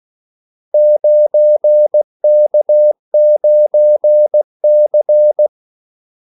Morse code, heard as a keyed tone, about 16 wpm, 600 Hz. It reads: 9K9C
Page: Kaldesignaler 16 wpm